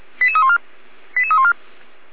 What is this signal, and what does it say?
Five-tone selective call (ZVEI), heard as quick runs of five tones, tones of about 80 ms. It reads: 88314 88314